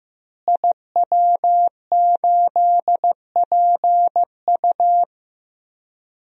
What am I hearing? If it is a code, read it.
Morse code, heard as a keyed tone, about 15 wpm, 700 Hz. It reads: IW8PU
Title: DXCC-100 Call Signs - 5 Characters - Repeated 2x - 15wpm